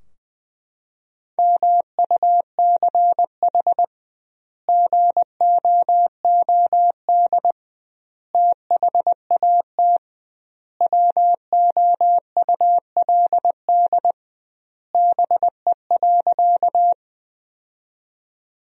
Morse code, heard as a keyed tone, about 20 wpm, 700 Hz. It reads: MUCH GOOD THAT WOULD BE.
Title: 2x Word-Spacing - Sentences from Top 200 Words - 20wpm